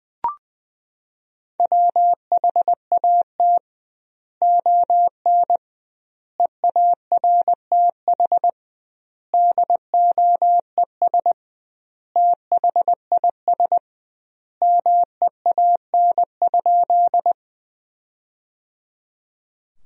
Morse code, heard as a keyed tone, about 20 wpm, 700 Hz. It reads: WHAT ON EARTH DOES THIS MEAN?
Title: 2x Word-Spacing - Sentences from Top 200 Words - 20wpm